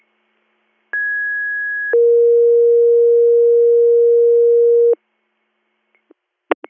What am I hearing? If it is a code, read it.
Two-tone page, heard as a one-second tone, then a three-second tone, about 1650 then 470 Hz. It is heard from 0.9 to 4.9 s.